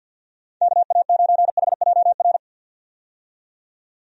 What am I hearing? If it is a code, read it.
Morse code, heard as a keyed tone, about 51 wpm, 700 Hz. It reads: KA9HJR
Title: USA Call Signs - 6 Characters - 50wpm